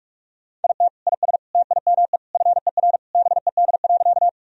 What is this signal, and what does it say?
Morse code, heard as a keyed tone, about 46 wpm, 700 Hz. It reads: IT IS TIME FER BED.
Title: Sentences from Top 500 Words and Abbreviations 45wpm